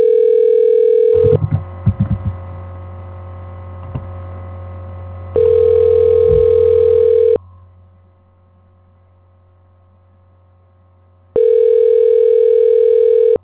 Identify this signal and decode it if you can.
Ringback tone (North America), 440 + 480 Hz.
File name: recordings / REe254fce d5dbd24345dc549fd87